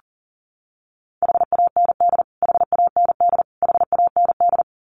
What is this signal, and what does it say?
Morse code, heard as a keyed tone, about 40 wpm, 700 Hz. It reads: HAND HAND HAND
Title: Top 500 Words Repeated 40wpm